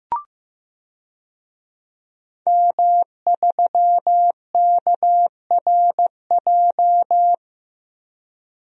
Morse code, heard as a keyed tone, about 15 wpm, 700 Hz. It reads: M3KRJ
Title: DXCC-100 Call Signs - 5 Characters - 15wpm